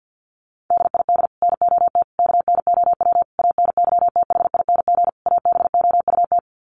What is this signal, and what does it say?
Morse code, heard as a keyed tone, about 50 wpm, 700 Hz. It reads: DID NOT KNOW ANYTHING ABOUT
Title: Sets of 5 Words Encore2 50wpm